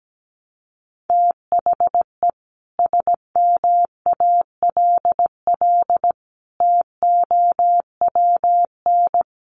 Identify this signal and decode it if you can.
Morse code, heard as a keyed tone, about 17 wpm, 700 Hz. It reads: THE SMALL TOWN
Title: Sentences from Top 1300 Words 17wpm